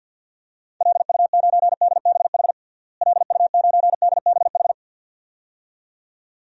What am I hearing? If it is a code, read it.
Morse code, heard as a keyed tone, about 47 wpm, 700 Hz. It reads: LU9DBH LU9DBH
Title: DXCC-100 Call Signs - 6 Characters - Repeated 2x - 50wpm